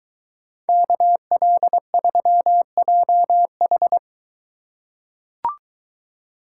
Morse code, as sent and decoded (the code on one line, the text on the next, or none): -.- .-.. ...-- .--- ....
KL3JH